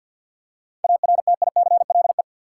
Morse code, heard as a keyed tone, about 50 wpm, 700 Hz. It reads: ARTICLE
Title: Seven Letter Words - 50wpm